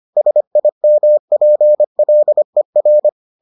Morse code, heard as a keyed tone, about 25 wpm, 600 Hz. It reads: SIMPLER